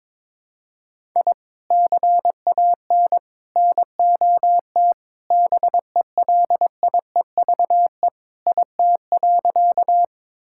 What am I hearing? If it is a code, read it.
Morse code, heard as a keyed tone, about 22 wpm, 700 Hz. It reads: I CAN NOT BELIEVE IT.
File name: practice / 22wpm / Sentences with 4-Word Phrases - 22wpm